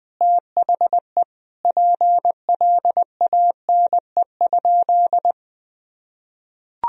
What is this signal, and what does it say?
Morse code, heard as a keyed tone, about 20 wpm, 700 Hz. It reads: THE PLANE?
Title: Sentences from Top 1000 Words - Review Part 3 - 20wpm